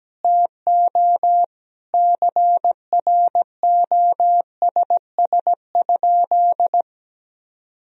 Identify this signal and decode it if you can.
Morse code, heard as a keyed tone, about 17 wpm, 700 Hz. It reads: TO CROSS?
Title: Sentences from Top 300 Words and Abbreviations 17wpm